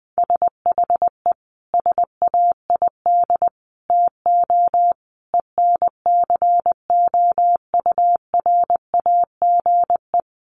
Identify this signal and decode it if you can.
Morse code, heard as a keyed tone, about 20 wpm, 700 Hz. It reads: SHE SAID TO ENCOURAGE
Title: Sentences from Top 1400 Words 20wpm